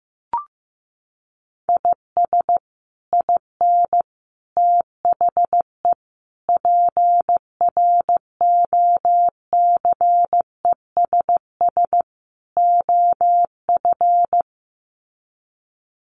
Morse code, heard as a keyed tone, about 15 wpm, 700 Hz. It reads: IS IN THE PROCESS OF
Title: Sets of 5 Words Encore2 15wpm